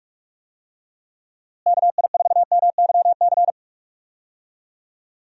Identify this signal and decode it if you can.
Morse code, heard as a keyed tone, about 48 wpm, 700 Hz. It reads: KI4MYC
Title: USA Call Signs - 6 Characters - Repeated 3x - 45wpm